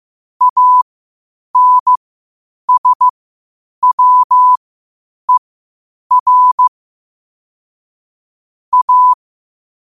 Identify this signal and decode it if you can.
Morse code, heard as a keyed tone, about 15 wpm, spaced as for 8 wpm, 1000 Hz. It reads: ANSWER A
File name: qotdfiles / 8wpm